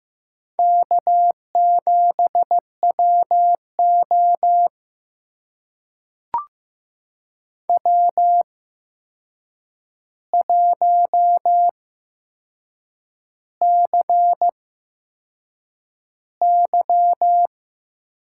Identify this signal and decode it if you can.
Morse code, heard as a keyed tone, about 15 wpm, spaced as for 6 wpm, 700 Hz. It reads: K7WO W1CY
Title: USA Call Signs - 4 Characters - 8x Character Spacing - 15wpm